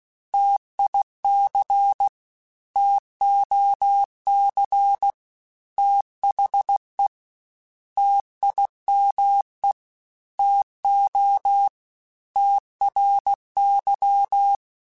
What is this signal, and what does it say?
Morse code, heard as a keyed tone, about 16 wpm, 800 Hz. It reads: TIC TOC THE TIME TO TRY
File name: T Runs / T Runs mp3s / T Runs @15